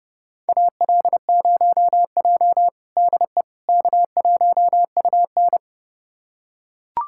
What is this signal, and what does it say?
Morse code, heard as a keyed tone, about 30 wpm, 700 Hz. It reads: AL0J DE K1UN
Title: USA Callsign DE Callsign - 4 Characters - 30wpm